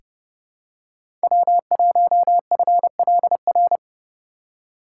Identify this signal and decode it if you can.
Morse code, heard as a keyed tone, about 30 wpm, 700 Hz. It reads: W1FLR
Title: USA Call Signs - 2x Character Spacing - 30wpm